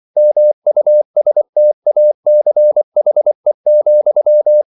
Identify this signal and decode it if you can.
Morse code, heard as a keyed tone, about 24 wpm, 600 Hz. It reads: MUSTACHE,